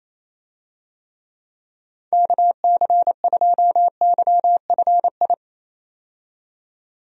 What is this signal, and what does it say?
Morse code, heard as a keyed tone, about 28 wpm, 700 Hz. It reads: KC2YFI
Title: USA Call Signs - 6 Characters - 28wpm